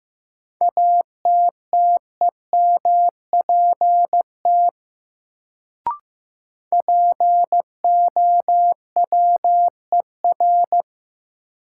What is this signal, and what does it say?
Morse code, heard as a keyed tone, about 15 wpm, 700 Hz. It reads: ATTEMPT POWER
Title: Sets of 4 Words - New Word Drill - 15wpm